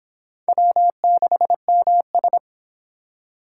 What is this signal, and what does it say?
Morse code, heard as a keyed tone, about 24 wpm, 700 Hz. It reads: W6MS